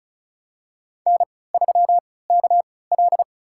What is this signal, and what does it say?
Morse code, heard as a keyed tone, about 34 wpm, spaced as for 20 wpm, 690 Hz. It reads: N3KL
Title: USA Call Signs - 3x Character Spacing - 35wpm